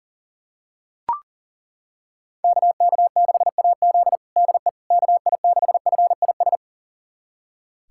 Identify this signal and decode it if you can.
Morse code, heard as a keyed tone, about 37 wpm, 700 Hz. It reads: KK6AZ DE KI6FIS
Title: USA Callsign DE Callsign 40wpm